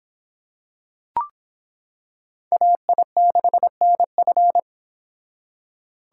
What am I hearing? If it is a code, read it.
Morse code, heard as a keyed tone, about 25 wpm, 700 Hz. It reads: AI6NF